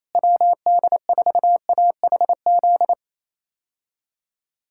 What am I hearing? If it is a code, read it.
Morse code, heard as a keyed tone, about 28 wpm, 700 Hz. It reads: WD4AHZ
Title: K1USN SST QSO Practice - 28wpm